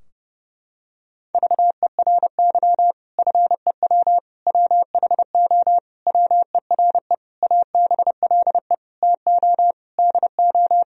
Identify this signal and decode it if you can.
Morse code, heard as a keyed tone, about 30 wpm, 700 Hz. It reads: VERY FEW WHO WERE ABLE TO DO
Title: Sentences from Top 1000 Words - Review Part 2 - 30wpm